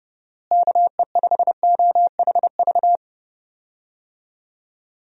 Morse code, heard as a keyed tone, about 30 wpm, 700 Hz. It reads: KE5OHV
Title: USA Call Signs - 6 Characters - Repeated 3x - 30wpm